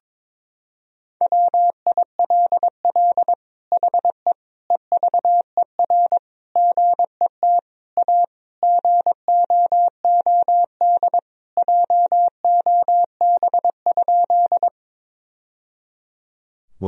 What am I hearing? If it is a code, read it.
Morse code, heard as a keyed tone, about 22 wpm, 700 Hz. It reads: WILL HE EVER GET A GOOD JOB?
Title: Sentences from Top 1000 Words - Review Part 3 - 22wpm